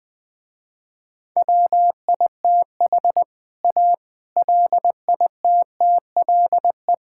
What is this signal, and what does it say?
Morse code, heard as a keyed tone, about 20 wpm, 700 Hz. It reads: WITH A LITTLE
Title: Sentences with 7-Word Phrases - Spoken Beginning - 20wpm